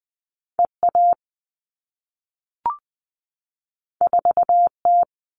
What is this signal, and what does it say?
Morse code, heard as a keyed tone, about 20 wpm, 700 Hz. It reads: EA 4T
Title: All Cut Numbers - 2-Digit 20wpm